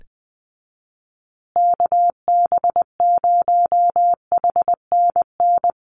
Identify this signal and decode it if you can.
Morse code, heard as a keyed tone, about 20 wpm, 700 Hz. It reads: KB0HNN
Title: USA Call Signs Repeated 2x 20wpm